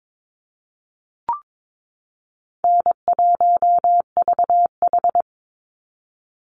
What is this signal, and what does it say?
Morse code, heard as a keyed tone, about 22 wpm, 700 Hz. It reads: N1VH
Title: Easy Call Signs Repeated 2x - 22wpm